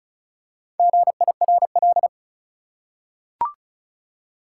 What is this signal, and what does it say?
Morse code, heard as a keyed tone, about 36 wpm, 700 Hz. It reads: GIRL
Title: Four Letter Words - 35wpm